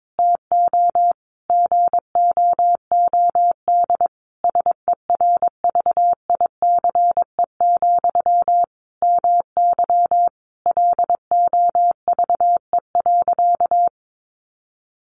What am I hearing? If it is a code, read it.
Morse code, heard as a keyed tone, about 22 wpm, 700 Hz. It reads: TO GOOD SERVICE, MY LOVE.
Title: Sentences from Top 1100 Words 22wpm